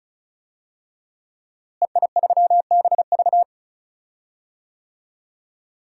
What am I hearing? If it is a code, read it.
Morse code, heard as a keyed tone, about 36 wpm, 700 Hz. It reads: EI3BV
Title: DXCC-100 Call Signs - 5 Characters - Repeated 3x - 35wpm